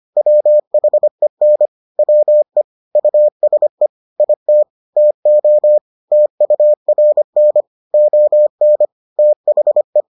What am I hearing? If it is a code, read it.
Morse code, heard as a keyed tone, about 25 wpm, 600 Hz. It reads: WHEN WE USE IT TO TURN ON THE